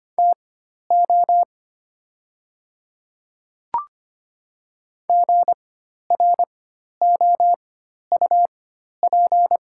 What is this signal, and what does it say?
Morse code, heard as a keyed tone, about 25 wpm, spaced as for 12 wpm, 700 Hz. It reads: TO GROUP
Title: 4x Character Spacing - Top 300 Words - 25wpm